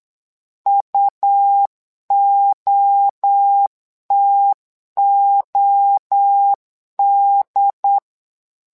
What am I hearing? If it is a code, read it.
Morse code, heard as a keyed tone, about 8 wpm, 800 Hz. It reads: UOTOD
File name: LEZIONI / 3 MWRO / 3 MWRO 8wpm 2606